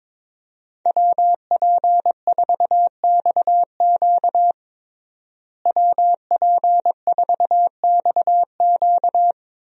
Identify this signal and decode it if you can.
Morse code, heard as a keyed tone, about 21 wpm, 700 Hz. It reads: WP4XQ WP4XQ
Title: USA Call Signs - 5 Characters - Repeated 2x - 22wpm